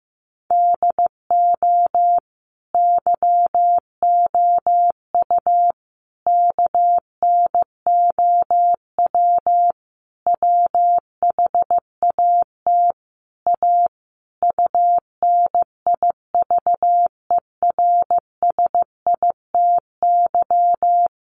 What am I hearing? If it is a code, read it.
Morse code, heard as a keyed tone, about 15 wpm, 700 Hz. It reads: DO YOU KNOW WHAT A UNIVERSITY
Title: Sentences from Top 1500 Words 15wpm